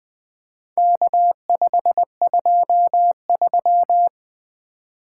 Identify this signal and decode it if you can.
Morse code, heard as a keyed tone, about 20 wpm, 700 Hz. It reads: K523
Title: Four Letter-Number Combinations - 20wpm